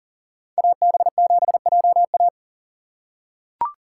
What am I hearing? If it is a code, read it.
Morse code, heard as a keyed tone, about 40 wpm, 700 Hz. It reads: AB7JA
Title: USA Call Signs - 5 Characters - 3x Character Spacing - 40wpm